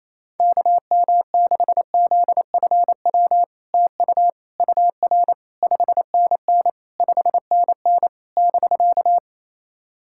Morse code, heard as a keyed tone, about 28 wpm, 700 Hz. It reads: KM6ZFW TU UR 5NN 5NN <BK>
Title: SOTA QSO Practice 28wpm